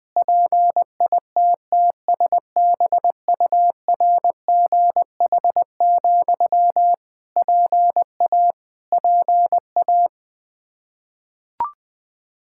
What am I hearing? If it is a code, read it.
Morse code, heard as a keyed tone, about 21 wpm, 700 Hz. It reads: PITTSBURGH, PA PA
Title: Practice QSO - Exchange 1 - Beginning - 20wpm